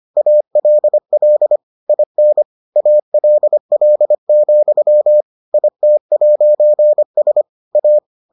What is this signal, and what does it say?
Morse code, heard as a keyed tone, about 25 wpm, 600 Hz. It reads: ALL IN ALL, IT'S A